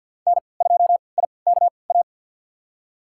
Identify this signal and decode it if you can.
Morse code, heard as a keyed tone, about 51 wpm, spaced as for 28 wpm, 700 Hz. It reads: N2IKA
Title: USA Call Signs - 5 Characters - 3x Character Spacing - 50wpm